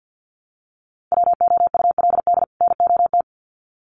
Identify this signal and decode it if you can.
Morse code, heard as a keyed tone, about 50 wpm, 700 Hz. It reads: WOULD NOT
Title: Sets of 2 Words 50wpm